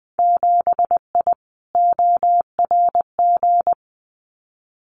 Morse code, heard as a keyed tone, about 20 wpm, 700 Hz. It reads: 7I ORG
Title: Winter Field Day Exchanges Encore 20wpm